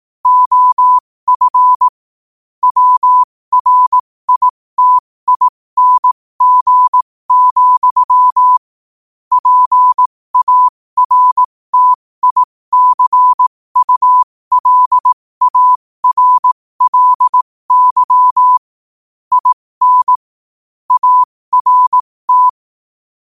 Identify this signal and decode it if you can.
Morse code, heard as a keyed tone, about 18 wpm, spaced as for 15 wpm, 1000 Hz. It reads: OF WRITING, PARTICULARLY IN ART
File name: qotdfiles / 15wpm